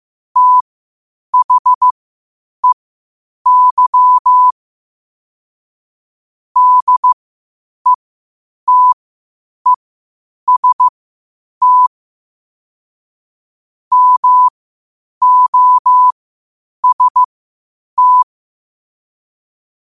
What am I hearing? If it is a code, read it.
Morse code, heard as a keyed tone, about 15 wpm, spaced as for 8 wpm, 1000 Hz. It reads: THEY DETEST MOST
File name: qotdfiles / 8wpm